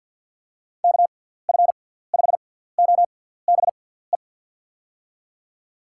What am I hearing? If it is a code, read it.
Morse code, heard as a keyed tone, about 53 wpm, spaced as for 17 wpm, 700 Hz. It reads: KF5OBE
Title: USA Call Signs - 6 Characters - 6x Character Spacing - 50wpm